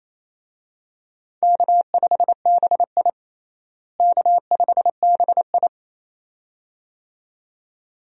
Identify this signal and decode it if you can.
Morse code, heard as a keyed tone, about 27 wpm, 700 Hz. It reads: K5BI K5BI
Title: USA Call Signs - 4 Characters - Repeated 2x - 28wpm